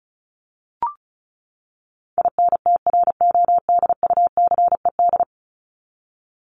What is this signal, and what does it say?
Morse code, heard as a keyed tone, about 34 wpm, 700 Hz. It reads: INTRODUCED
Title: ED Suffix - Top 100 Words with Base Words 35wpm